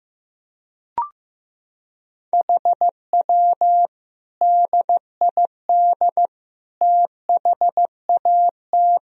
Morse code, heard as a keyed tone, about 15 wpm, 700 Hz. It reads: HW DID THAT
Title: Sentences from Top 200 Words and Abbreviations 15wpm